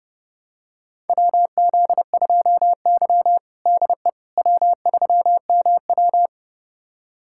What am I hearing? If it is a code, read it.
Morse code, heard as a keyed tone, about 32 wpm, 700 Hz. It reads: WZ2Y DE W3MW
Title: USA Callsign DE Callsign - 4 Characters - 30wpm